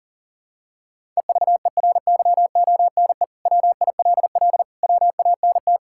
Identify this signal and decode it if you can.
Morse code, heard as a keyed tone, about 40 wpm, 700 Hz. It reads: EVERYONE WILL WANT
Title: Sentences from Top 1000 Words - Review Part 5 - 40wpm